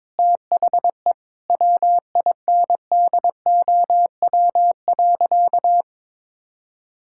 Morse code, heard as a keyed tone, about 22 wpm, 700 Hz. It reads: THE WINDOW.
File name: practice / 22wpm / Sentences with 2-Word Phrases - 22wpm